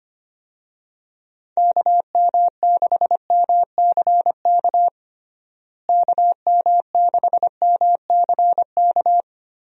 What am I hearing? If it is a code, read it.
Morse code, heard as a keyed tone, about 25 wpm, 700 Hz. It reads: KM6MCK KM6MCK
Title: USA Call Signs - 6 Characters - Repeated 2x - 25wpm